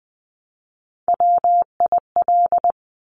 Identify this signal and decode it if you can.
Morse code, heard as a keyed tone, about 20 wpm, 700 Hz. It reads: WIL
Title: Top Three Letter Combinations in Words 20wpm